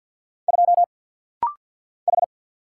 Morse code, heard as a keyed tone, about 50 wpm, 700 Hz. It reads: 2 H
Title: Single Letter-Number Alt - Rapid-Fire 50wpm